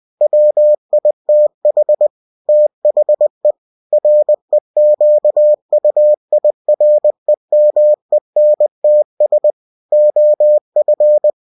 Morse code, heard as a keyed tone, about 20 wpm, 600 Hz. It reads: WITH THE REQUIREMENTS OF
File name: cw-books / winnie-the-pooh / winnie-20wpm-page-0115